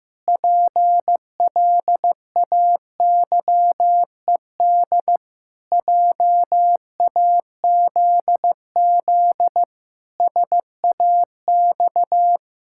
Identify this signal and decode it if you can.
Morse code, heard as a keyed tone, about 16 wpm, 700 Hz. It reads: PLAYED JAZZ SAX
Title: Pangram Challenge v2 - 15wpm